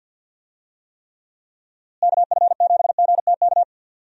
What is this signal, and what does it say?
Morse code, heard as a keyed tone, about 47 wpm, 700 Hz. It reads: KF7GTK
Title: USA Call Signs - 6 Characters - Repeated 2x - 50wpm